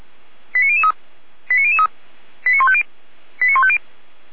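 Five-tone selective call (ZVEI), heard as quick runs of five tones, tones of about 70 ms. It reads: 89003 89003 89150 89150
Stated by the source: ZVEI five-tone call 89078 received by SDR.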